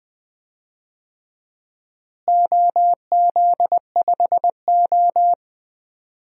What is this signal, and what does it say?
Morse code, heard as a keyed tone, about 20 wpm, 700 Hz. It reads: OZ5O
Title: DXCC-100 Call Signs - 4 Characters - Repeated 3x - 20wpm